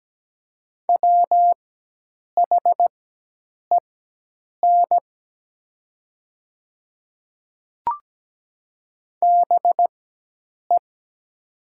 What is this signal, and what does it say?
Morse code, heard as a keyed tone, about 17 wpm, spaced as for 8 wpm, 700 Hz. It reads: WHEN BE
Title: 4x Character Spacing - Top 100 Words - 17wpm